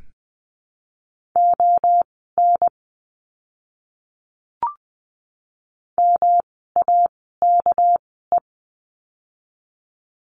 Morse code, heard as a keyed tone, about 20 wpm, spaced as for 15 wpm, 700 Hz. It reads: ON MAKE
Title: 230 Lesson 37f - Letter x - Words 20wpm